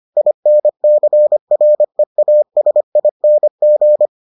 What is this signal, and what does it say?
Morse code, heard as a keyed tone, about 25 wpm, 600 Hz. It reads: INCREASING